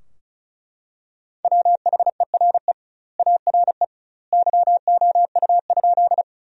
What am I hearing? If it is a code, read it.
Morse code, heard as a keyed tone, about 36 wpm, 700 Hz. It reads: WHERE ARE YOU?
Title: 2x Word-Spacing - Sentences from Top 200 Words - 35wpm